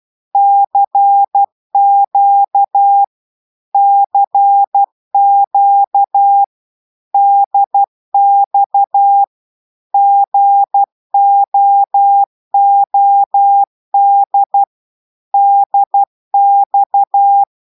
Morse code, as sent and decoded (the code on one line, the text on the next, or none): -.-. --.- / -.-. --.- / -.. -..- / --. --- --- -.. / -.. -..-
CQ CQ DX GOOD DX